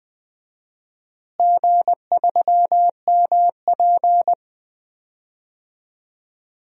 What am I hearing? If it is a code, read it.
Morse code, heard as a keyed tone, about 21 wpm, 700 Hz. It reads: G3MP
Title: DXCC-100 Call Signs - 4 Characters - Repeated 2x - 20wpm